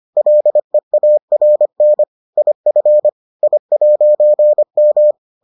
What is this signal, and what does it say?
Morse code, heard as a keyed tone, about 25 wpm, 600 Hz. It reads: LEARN IF I'M